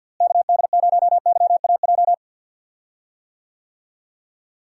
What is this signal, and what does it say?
Morse code, heard as a keyed tone, about 48 wpm, 700 Hz. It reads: KD0YAJ
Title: USA Call Signs - 6 Characters - Repeated 3x - 50wpm